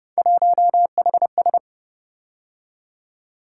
Morse code, heard as a keyed tone, about 30 wpm, 700 Hz. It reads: 1HS